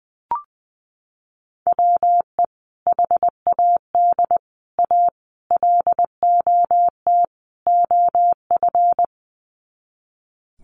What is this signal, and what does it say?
Morse code, heard as a keyed tone, about 20 wpm, 700 Hz. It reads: WE HAD A LOT OF
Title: Sets of 5 Words Encore 20wpm